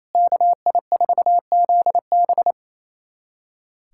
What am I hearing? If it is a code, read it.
Morse code, heard as a keyed tone, about 28 wpm, 690 Hz. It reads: KI4ZB
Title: USA Callsign DE Callsign - 5 Characters - 28wpm